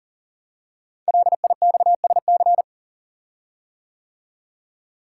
Morse code, heard as a keyed tone, about 42 wpm, 690 Hz. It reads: LIXSC
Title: Code Groups - 5 Letters - 40wpm